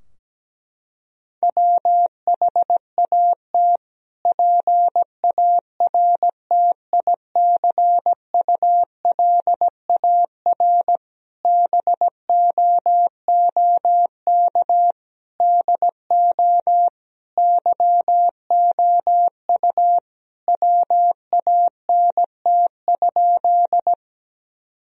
Morse code, heard as a keyed tone, about 17 wpm, 700 Hz. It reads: WHAT PARTICULAR BOOK DO YOU WANT?
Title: Sentences from Top 1000 Words - Review Part 4 - 17wpm